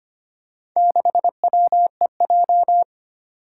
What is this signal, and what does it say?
Morse code, heard as a keyed tone, about 25 wpm, 700 Hz. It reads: 6WEJ
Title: Four Challenging Letter-Number Combinations 25wpm